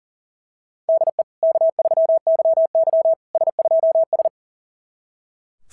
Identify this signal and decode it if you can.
Morse code, heard as a keyed tone, about 37 wpm, 660 Hz. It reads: DE K3YY S2S
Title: SOTA QSO Practice 40wpm